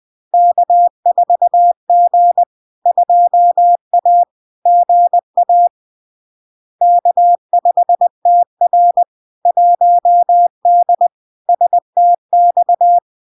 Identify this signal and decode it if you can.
Morse code, heard as a keyed tone, about 20 wpm, 700 Hz. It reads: K4G 2A GA K5TR 1D STX
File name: LARK-FD-2022-Log-in-CW-20wpm-700-Hz